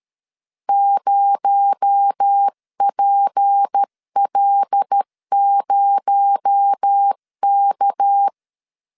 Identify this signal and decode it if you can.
Morse code, heard as a keyed tone, about 13 wpm, 800 Hz. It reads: 0PL0K
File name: LEZIONI / 7 GF9Z / 7 GF 12wpm 2122